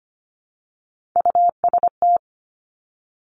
Morse code, heard as a keyed tone, about 25 wpm, 700 Hz. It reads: UST
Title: Three Challenging Letter-Number Combinations 25wpm